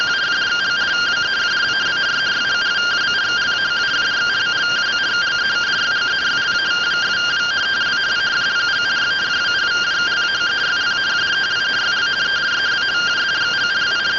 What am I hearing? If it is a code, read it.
Two-tone teleprinter signal, about 170 Hz shift.